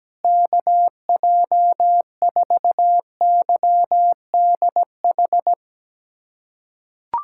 Morse code, heard as a keyed tone, about 17 wpm, 700 Hz. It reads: KJ4YDH
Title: USA Callsign DE Callsign 17wpm